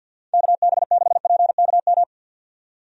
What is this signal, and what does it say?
Morse code, heard as a keyed tone, about 51 wpm, 700 Hz. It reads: KB6PKM
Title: USA Call Signs - 6 Characters - 50wpm